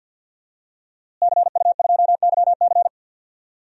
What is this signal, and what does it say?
Morse code, heard as a keyed tone, about 52 wpm, 700 Hz. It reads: CU2YC
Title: DXCC-100 Call Signs - 5 Characters - Repeated 2x - 50wpm